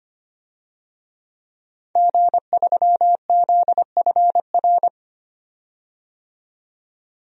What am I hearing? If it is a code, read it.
Morse code, heard as a keyed tone, about 25 wpm, 700 Hz. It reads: G3ZFR
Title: DXCC-100 Call Signs - 5 Characters - 25wpm